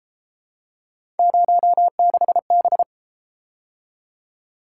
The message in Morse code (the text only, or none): 06B